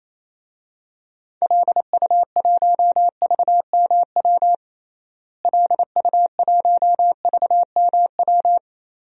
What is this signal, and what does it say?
Morse code, heard as a keyed tone, about 28 wpm, 700 Hz. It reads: LU1VMW LU1VMW
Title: DXCC-100 Call Signs - 6 Characters - Repeated 3x - 28wpm